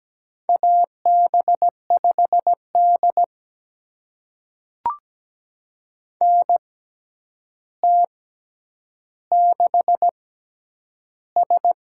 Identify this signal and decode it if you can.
Morse code, heard as a keyed tone, about 17 wpm, spaced as for 9 wpm, 700 Hz. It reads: AB5D NT6S
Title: USA Call Signs - 4 Characters - 6x Character Spacing - 17wpm